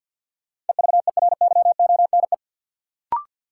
Morse code, heard as a keyed tone, about 46 wpm, 700 Hz. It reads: EVERYONE
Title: Sentences from Top 800 Words - New Word Drill - 50wpm